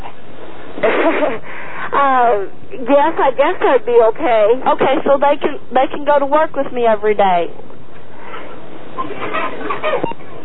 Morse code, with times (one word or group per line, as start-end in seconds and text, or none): none